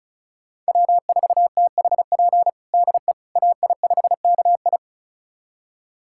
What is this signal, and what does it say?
Morse code, heard as a keyed tone, about 36 wpm, 700 Hz. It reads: W4THP DE AI5KI
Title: USA Callsign DE Callsign - 5 Characters - 35wpm